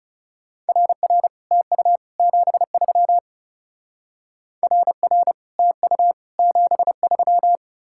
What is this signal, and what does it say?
Morse code, heard as a keyed tone, about 36 wpm, 700 Hz. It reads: RR TU 73 RR TU 73
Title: Speed-Racing Common QSO Phrases Encore - 30wpm